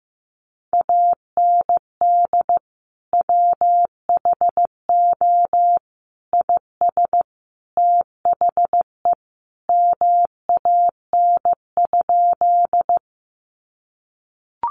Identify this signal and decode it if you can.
Morse code, heard as a keyed tone, about 14 wpm, 700 Hz. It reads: AND WHO IS THE MAN?
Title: Sentences from Top 200 Words 15wpm